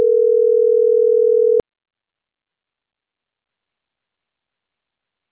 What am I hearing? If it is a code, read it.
Ringback tone (North America), 440 + 480 Hz.